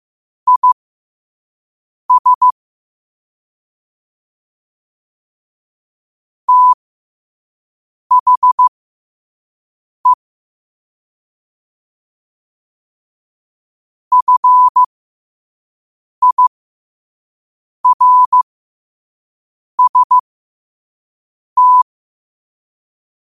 Morse code, heard as a keyed tone, about 15 wpm, spaced as for 5 wpm, 1000 Hz. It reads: IS THE FIRST
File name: qotdfiles / 5wpm